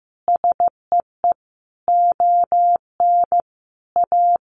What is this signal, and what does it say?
Morse code, heard as a keyed tone, about 15 wpm, 700 Hz. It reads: SEE ON A
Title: Sentences from Top 1300 Words 15wpm